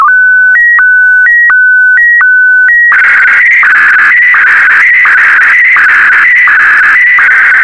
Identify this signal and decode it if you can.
SSTV (slow-scan television), PD 90.
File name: sstv / W2180-VK4MSL